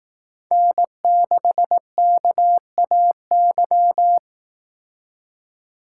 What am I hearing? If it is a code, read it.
Morse code, heard as a keyed tone, about 18 wpm, 700 Hz. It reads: N6KAY
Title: Speed Racing - USA Call Signs 15wpm